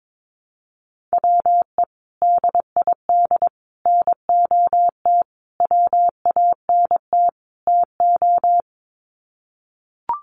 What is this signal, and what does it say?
Morse code, heard as a keyed tone, about 22 wpm, 700 Hz. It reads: WE DID NOT WANT TO
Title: Sets of 5 Words Encore 22wpm